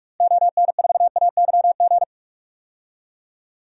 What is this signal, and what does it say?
Morse code, heard as a keyed tone, about 47 wpm, 690 Hz. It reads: ON4AYG
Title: DXCC-100 Call Signs - 6 Characters - Repeated 2x - 45wpm